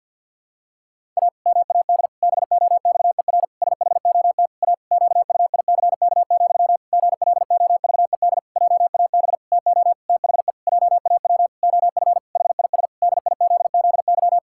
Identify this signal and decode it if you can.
Morse code, heard as a keyed tone, about 50 wpm, 700 Hz. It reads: A MAD BOXER SHOT A QUICK, GLOVED JAB TO THE JAW OF HIS DIZZY
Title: Pangram Challenge v2 - 50wpm